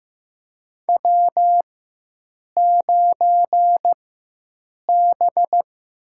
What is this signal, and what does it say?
Morse code, heard as a keyed tone, about 15 wpm, spaced as for 7 wpm, 700 Hz. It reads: W9B
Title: USA Call Signs - 4x Character Spacing - 15wpm